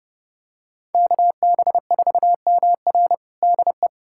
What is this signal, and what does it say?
Morse code, heard as a keyed tone, about 30 wpm, 700 Hz. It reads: KB4MR DE
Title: USA Callsign DE Callsign - 5 Characters - 30wpm